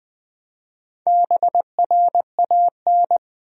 Morse code, heard as a keyed tone, about 20 wpm, 700 Hz. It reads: BRAN
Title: USA Common Names - Max 4 Letters - 20wpm